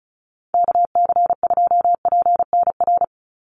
Morse code, heard as a keyed tone, about 34 wpm, 700 Hz. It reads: KC2PNR
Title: USA Call Signs 35wpm